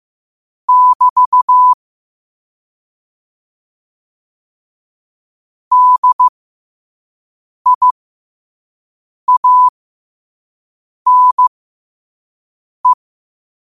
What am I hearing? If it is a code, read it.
Morse code, heard as a keyed tone, about 15 wpm, spaced as for 5 wpm, 1000 Hz. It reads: <BT> DIANE
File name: qotdfiles / 5wpm